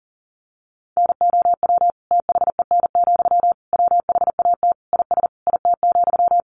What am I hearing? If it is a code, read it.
Morse code, heard as a keyed tone, about 40 wpm, 700 Hz. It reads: NOW THEN, WHAT IS IT,
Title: Sentences from Top 1100 Words 40wpm